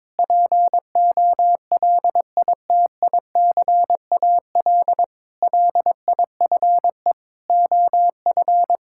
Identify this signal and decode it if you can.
Morse code, heard as a keyed tone, about 22 wpm, 700 Hz. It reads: POLITICAL LIFE OF
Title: Sentences from Top 1000 Words - Review Part 5 - 22wpm